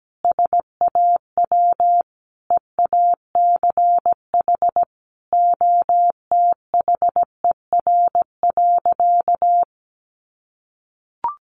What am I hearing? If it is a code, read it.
Morse code, heard as a keyed tone, about 17 wpm, 700 Hz. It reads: SAW EACH OTHER.
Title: Sentences from Top 1400 Words 17wpm